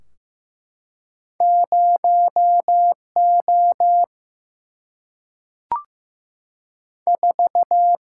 Morse code, heard as a keyed tone, about 15 wpm, 700 Hz. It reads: 0 O 4
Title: Single-Letter Single-Number 15wpm